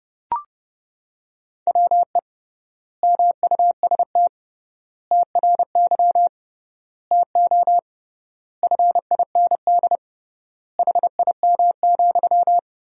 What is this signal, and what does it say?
Morse code, heard as a keyed tone, about 30 wpm, 700 Hz. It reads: WE MUST TRY TO FIND HIM,
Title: 3x Word-Spacing - Sentences from Top 400 Words - 30wpm